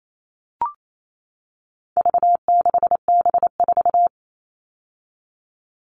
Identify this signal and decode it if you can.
Morse code, heard as a keyed tone, about 28 wpm, 700 Hz. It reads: V6B4
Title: All Cut Numbers - 4-Digit 28wpm